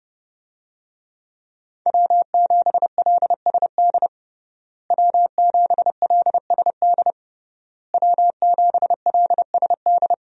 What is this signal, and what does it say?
Morse code, heard as a keyed tone, about 30 wpm, 700 Hz. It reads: W7LSD W7LSD W7LSD
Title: USA Call Signs - 5 Characters - Repeated 3x - 30wpm